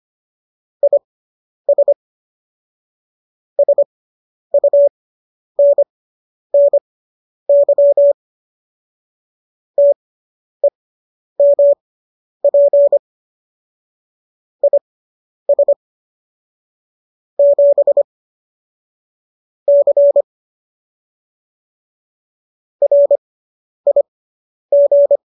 Morse code, heard as a keyed tone, about 25 wpm, spaced as for 10 wpm, 570 Hz. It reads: IS SUNNY TEMP IS 7 C RIG